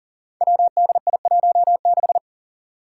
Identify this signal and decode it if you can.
Morse code, heard as a keyed tone, about 40 wpm, 700 Hz. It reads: WDI16
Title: Five Challenging Letter-Number Combinations 40wpm